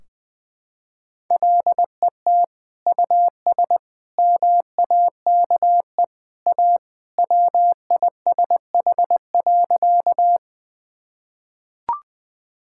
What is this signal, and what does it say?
Morse code, heard as a keyed tone, about 21 wpm, 700 Hz. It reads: LET US MAKE A WISH.
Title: Sentences from Top 1600 Words 20wpm